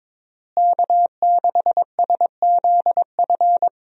Morse code, heard as a keyed tone, about 22 wpm, 700 Hz. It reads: K6SZF
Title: USA Call Signs - 5 Characters - Repeated 3x - 22wpm